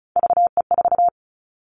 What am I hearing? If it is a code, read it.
Morse code, heard as a keyed tone, about 34 wpm, 700 Hz. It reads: VE4
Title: Callsign Prefixes for Top 25 DXCC - Repeated 3x - 35wpm